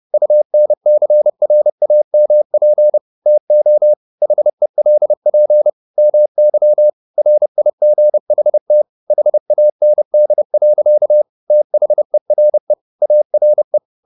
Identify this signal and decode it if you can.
Morse code, heard as a keyed tone, about 30 wpm, 600 Hz. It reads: UNCRAMP TO HELP MY RIGHT HAND. THERE ARE
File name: the old man and the sea / the old man and the sea 1 30wpm